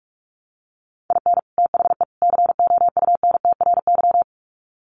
Morse code, heard as a keyed tone, about 44 wpm, 700 Hz. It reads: IN THE COUNTRY
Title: Sets of 3 Words 45wpm